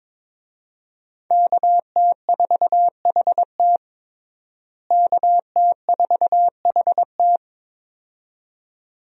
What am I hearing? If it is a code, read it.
Morse code, heard as a keyed tone, about 22 wpm, 700 Hz. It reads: KT4HT KT4HT
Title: USA Call Signs - 5 Characters - Repeated 2x - 22wpm